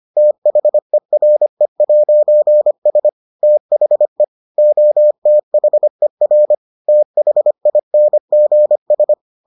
Morse code, heard as a keyed tone, about 25 wpm, 600 Hz. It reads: THERE'S THE OTHER THINGS